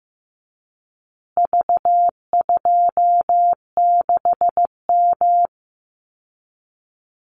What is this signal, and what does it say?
Morse code, heard as a keyed tone, about 15 wpm, 700 Hz. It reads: V26M
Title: Call Signs Repeated 2x - 15wpm